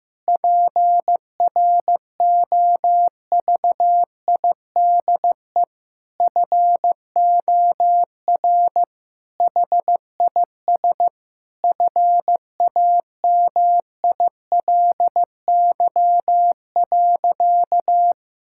Morse code, heard as a keyed tone, about 15 wpm, 700 Hz. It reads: PROVIDE FOR HIS FAMILY.
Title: Sentences from Top 1000 Words - Review Part 5 - 15wpm